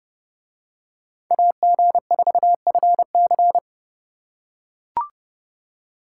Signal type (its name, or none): Morse code